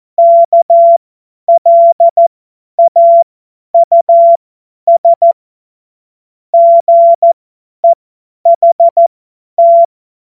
Morse code, heard as a keyed tone, about 14 wpm, spaced as for 10 wpm, 680 Hz. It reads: KLAUS GEHT
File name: Morsekurs / Witze / Witz 02. November 2021, 50 F70